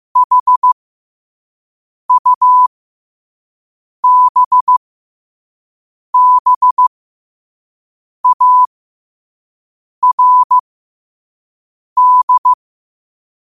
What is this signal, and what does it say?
Morse code, heard as a keyed tone, about 15 wpm, spaced as for 5 wpm, 1000 Hz. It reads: HUBBARD
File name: qotdfiles / 5wpm